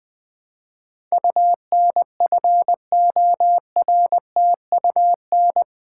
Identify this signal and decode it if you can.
Morse code, heard as a keyed tone, about 20 wpm, 700 Hz. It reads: UNFORTUN
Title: Build A Word - 4 Letters At A Time - 11-Letter Words - 20wpm